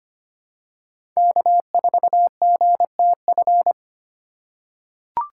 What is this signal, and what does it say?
Morse code, heard as a keyed tone, about 25 wpm, 700 Hz. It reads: K4GTF
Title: USA Call Signs - 5 Characters - 8x Character Spacing - 25wpm